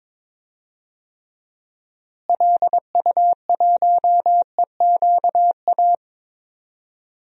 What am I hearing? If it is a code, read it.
Morse code, heard as a keyed tone, about 22 wpm, 700 Hz. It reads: LU1EQA
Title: DXCC-100 Call Signs - 6 Characters - Repeated 3x - 22wpm